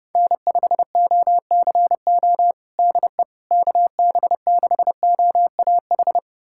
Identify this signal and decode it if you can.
Morse code, heard as a keyed tone, about 30 wpm, 700 Hz. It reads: N5OCO DE KB6OAH
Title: USA Callsign DE Callsign 30wpm